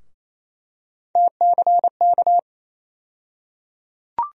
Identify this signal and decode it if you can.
Morse code, heard as a keyed tone, about 28 wpm, 700 Hz. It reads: TCK
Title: USA Call Sign Postfixes - 3 Letters 28wpm